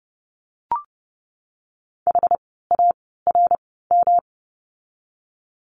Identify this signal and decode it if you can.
Morse code, heard as a keyed tone, about 30 wpm, spaced as for 17 wpm, 700 Hz. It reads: HARM
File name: course / 30:10wpm / 108 Lesson 18f - Letter m - Words 30wpm